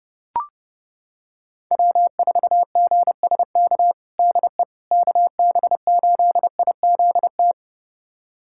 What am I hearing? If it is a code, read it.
Morse code, heard as a keyed tone, about 32 wpm, 700 Hz. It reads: W4GSK DE KB8IZT
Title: USA Callsign DE Callsign 30wpm